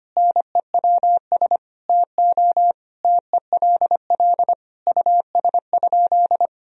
Morse code, heard as a keyed tone, about 25 wpm, 700 Hz. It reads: NEWS TO TELL US?
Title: Sentences from Top 1000 Words - Review Part 4 - 25wpm